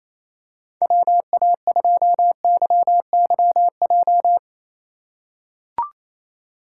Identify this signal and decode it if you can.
Morse code, heard as a keyed tone, about 28 wpm, 700 Hz. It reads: WA2YYJ
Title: USA Call Signs - 6 Characters - 4x Character Spacing - 28wpm